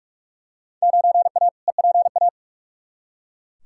Morse code, heard as a keyed tone, about 42 wpm, 700 Hz. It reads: 9A EPA
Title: Field Day Exchanges Encore - 45wpm